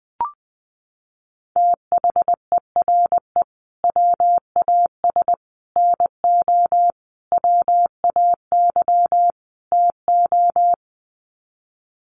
Morse code, heard as a keyed tone, about 19 wpm, 700 Hz. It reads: THERE WAS NO WAY TO
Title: Sets of 5 Words Encore 20wpm